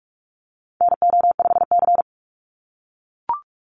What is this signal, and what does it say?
Morse code, heard as a keyed tone, about 44 wpm, 700 Hz. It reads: NO5C
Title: USA Call Signs 45wpm